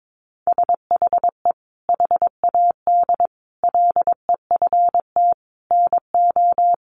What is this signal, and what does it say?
Morse code, heard as a keyed tone, about 22 wpm, 700 Hz. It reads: SHE HAD LEFT NO
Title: Sentences from Top 1100 Words 22wpm